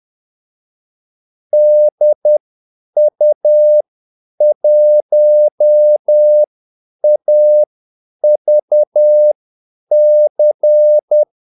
Morse code, heard as a keyed tone, about 10 wpm, spaced as for 8 wpm, 600 Hz. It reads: DU1AVC
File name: Callsigns-10-8wpm